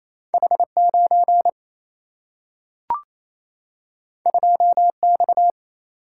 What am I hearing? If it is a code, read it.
Morse code, heard as a keyed tone, about 28 wpm, 700 Hz. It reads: H 9 2 X
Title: Single-Letter Single-Number 28wpm